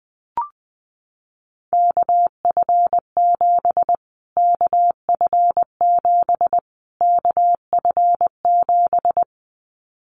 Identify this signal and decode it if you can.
Morse code, heard as a keyed tone, about 20 wpm, 700 Hz. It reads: KF7 KF7 KF7
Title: Callsign Prefixes for Top 25 DXCC - Repeated 3x - 20wpm